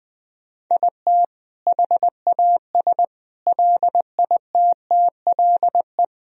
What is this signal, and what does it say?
Morse code, heard as a keyed tone, about 20 wpm, 700 Hz. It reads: IT HAS LITTLE
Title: Sentences from Top 1000 Words - Review Part 4 - 20wpm